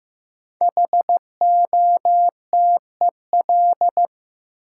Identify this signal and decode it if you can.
Morse code, heard as a keyed tone, about 15 wpm, 700 Hz. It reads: HOTEL
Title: Sentences from Top 1600 Words - New Word Drill - 15wpm